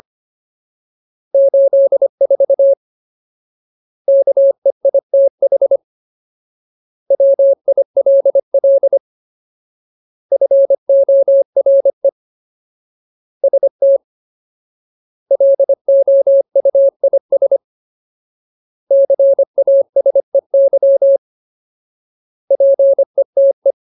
Morse code, heard as a keyed tone, about 25 wpm, 550 Hz. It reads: 84 KEITH WILL FORE ST LOUIS CASEY PETE
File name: Ultimate Word List / 25WPM Ultimate